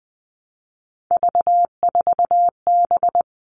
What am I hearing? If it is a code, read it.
Morse code, heard as a keyed tone, about 20 wpm, 700 Hz. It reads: V4B
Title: All Cut Numbers - 3-Digit 20wpm